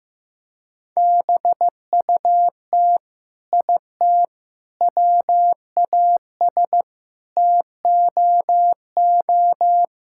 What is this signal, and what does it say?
Morse code, heard as a keyed tone, about 15 wpm, 700 Hz. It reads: BUT IT WAS TOO
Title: Sentences from Top 1600 Words 15wpm